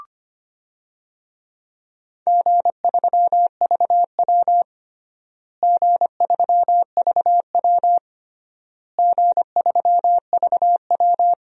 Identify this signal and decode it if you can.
Morse code, heard as a keyed tone, about 25 wpm, 700 Hz. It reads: G3VW G3VW G3VW
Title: DXCC-100 Call Signs - 4 Characters - Repeated 3x - 25wpm